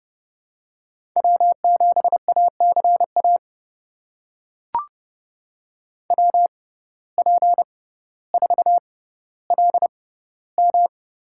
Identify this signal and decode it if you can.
Morse code, heard as a keyed tone, about 30 wpm, spaced as for 15 wpm, 700 Hz. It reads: W7ACA WP4LM
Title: USA Call Signs - 5 Characters - 6x Character Spacing - 30wpm